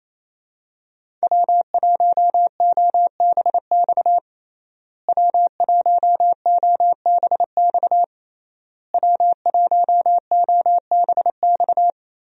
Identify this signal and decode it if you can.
Morse code, heard as a keyed tone, about 28 wpm, 700 Hz. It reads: W1OBX W1OBX W1OBX
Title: USA Call Signs - 5 Characters - Repeated 3x - 28wpm